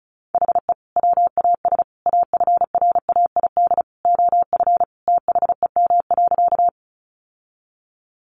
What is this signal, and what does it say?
Morse code, heard as a keyed tone, about 34 wpm, 700 Hz. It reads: HE WAS AFRAID OF THEM.
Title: Sentences from Top 1500 Words 35wpm